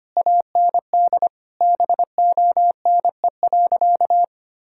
Morse code, heard as a keyed tone, about 25 wpm, 700 Hz. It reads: AND BONE.
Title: Sentences from Top 1600 Words 25wpm